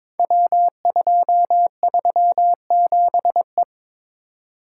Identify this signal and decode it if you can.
Morse code, heard as a keyed tone, about 23 wpm, 700 Hz. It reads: W237E
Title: Code Groups - 5 Letter:Numbers - 22wpm